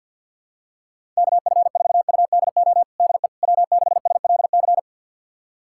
Morse code, heard as a keyed tone, about 48 wpm, 700 Hz. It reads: KF4UDO DE W6SLC
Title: USA Callsign DE Callsign 50wpm